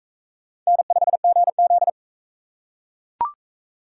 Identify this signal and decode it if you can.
Morse code, heard as a keyed tone, about 44 wpm, 700 Hz. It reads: N5GZ